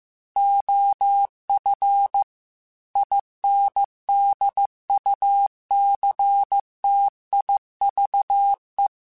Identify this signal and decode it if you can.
Morse code, heard as a keyed tone, about 15 wpm, 800 Hz. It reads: OF INDUCTIVE
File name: Technical Subjects / Technical Subjects-mp3s / XL @13